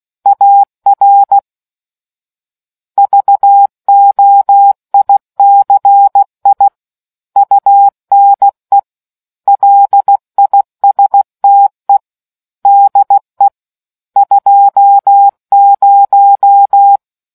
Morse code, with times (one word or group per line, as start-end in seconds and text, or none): 0.3-1.4 AR
3.0-6.7 VOICI
7.4-8.8 UNE
9.5-12.0 LISTE
12.7-13.5 DE
14.2-17.0 20